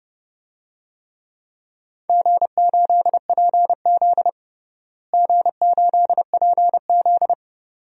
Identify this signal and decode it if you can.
Morse code, heard as a keyed tone, about 29 wpm, 700 Hz. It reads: G8PZ G8PZ
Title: DXCC-100 Call Signs - 4 Characters - Repeated 2x - 30wpm